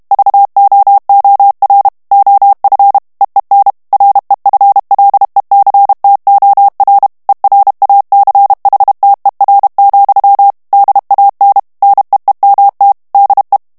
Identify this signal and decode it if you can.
Morse code, heard as a keyed tone, about 32 wpm, 800 Hz. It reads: VOOR OF EEN REFLECTOR ERACHTER, DAN NEEMT DE
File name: HOLLANDAIS texte clair 1 vitesse 30 mots minute